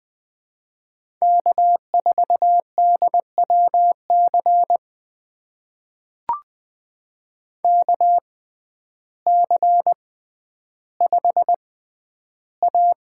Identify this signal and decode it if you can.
Morse code, heard as a keyed tone, about 21 wpm, spaced as for 11 wpm, 700 Hz. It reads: K4DWC KC5A
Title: USA Call Signs - 5 Characters - 6x Character Spacing - 20wpm